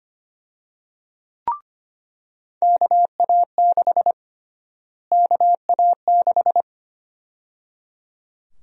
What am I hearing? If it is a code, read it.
Morse code, heard as a keyed tone, about 25 wpm, 700 Hz. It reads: KA6 KA6
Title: USA Call Sign Prefixes - Repeated 2x - 25wpm